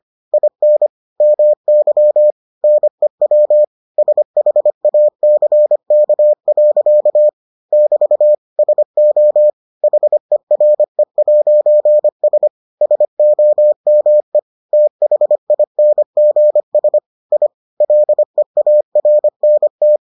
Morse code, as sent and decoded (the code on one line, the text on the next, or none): .. -. / -- -.-- / -. .-- / ... .... .- -.-. -.- .-.-.- / -...- / ... --- / .... .-. .----. ... / ... --- -- . / - .... .. -. --. ... / .. / .-.. .- .-. -. -
IN MY NEW SHACK. <BT> SO HERE'S SOME THINGS I LEARNT